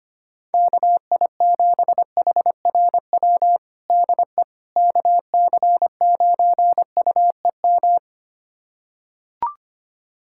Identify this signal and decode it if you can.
Morse code, heard as a keyed tone, about 25 wpm, 700 Hz. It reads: KI7HRW DE KC9UEM
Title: USA Callsign DE Callsign - 6 Characters - 25wpm